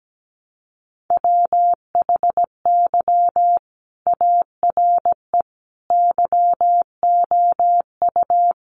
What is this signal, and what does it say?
Morse code, heard as a keyed tone, about 17 wpm, 700 Hz. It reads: WHY ARE YOU
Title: Sentences from Top 800 Words 17wpm